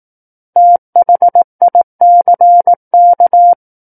Morse code, heard as a keyed tone, about 18 wpm, 700 Hz. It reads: THICK
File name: 18 WPM